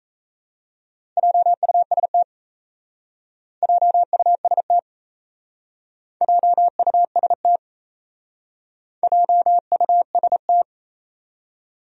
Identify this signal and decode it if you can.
Morse code, heard as a keyed tone, about 35 wpm, 700 Hz. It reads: JUST JUST JUST JUST
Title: Speed-Racing Top 200 Words 28wpm